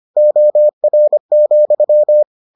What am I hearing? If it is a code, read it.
Morse code, heard as a keyed tone, about 25 wpm, 600 Hz. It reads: OR,